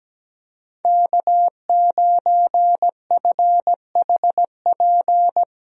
Morse code, heard as a keyed tone, about 17 wpm, 700 Hz. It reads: K9FHP
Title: USA Call Signs - 5 Characters - Repeated 3x - 17wpm